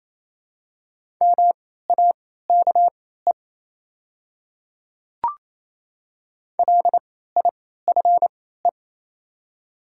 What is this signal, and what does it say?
Morse code, heard as a keyed tone, about 28 wpm, spaced as for 16 wpm, 700 Hz. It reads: MAKE LIFE
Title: 3x Character Spacing - Top 300 Words - 28wpm